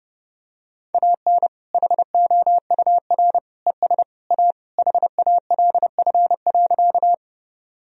Morse code, heard as a keyed tone, about 32 wpm, 700 Hz. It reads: AN HOUR ES A HALF.
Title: Sentences from Top 400 Words and Abbreviations 30wpm